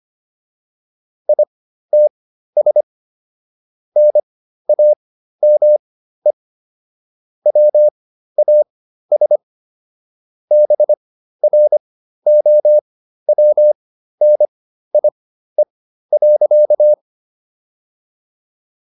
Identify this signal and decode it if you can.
Morse code, heard as a keyed tone, about 25 wpm, spaced as for 13 wpm, 600 Hz. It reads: ITS NAME WAS BROWNIE.